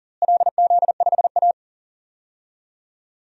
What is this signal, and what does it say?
Morse code, heard as a keyed tone, about 40 wpm, 700 Hz. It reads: LZ5A
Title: DXCC-100 Call Signs - 4 Characters - Repeated 3x - 40wpm